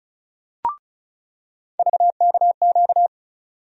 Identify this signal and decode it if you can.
Morse code, heard as a keyed tone, about 36 wpm, 700 Hz. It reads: VKQ
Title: USA Call Sign Postfixes - 3 Letters 35wpm